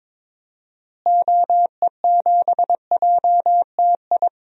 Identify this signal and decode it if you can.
Morse code, heard as a keyed tone, about 22 wpm, 700 Hz. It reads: OE7JTI
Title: DXCC-100 Call Signs - 6 Characters - Repeated 3x - 22wpm